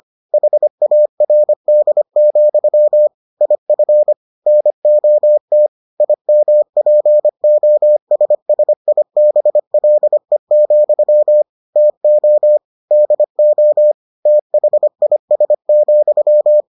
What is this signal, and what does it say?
Morse code, heard as a keyed tone, about 25 wpm, 600 Hz. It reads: HARD, IF NOT IMPOSSIBLE, TO DO THIS,